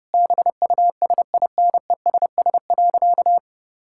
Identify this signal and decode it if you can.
Morse code, heard as a keyed tone, about 32 wpm, 700 Hz. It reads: BUSINESS.
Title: Sentences from Top 1000 Words - Review Part 5 - 30wpm